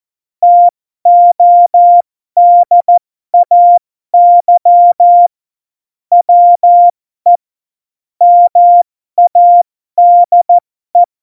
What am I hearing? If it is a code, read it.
Morse code, heard as a keyed tone, about 14 wpm, spaced as for 12 wpm, 700 Hz. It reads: TODAY WE MADE
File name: Morsekurs / Witze